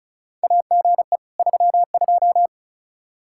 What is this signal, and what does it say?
Morse code, heard as a keyed tone, about 36 wpm, 700 Hz. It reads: AGE 32
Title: Practice QSO - Exchange 3 - More Info - 35wpm